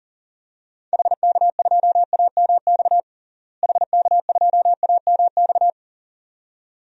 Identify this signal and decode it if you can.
Morse code, heard as a keyed tone, about 40 wpm, 700 Hz. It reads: HK2AMX HK2AMX
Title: DXCC-100 Call Signs - 6 Characters - Repeated 2x - 40wpm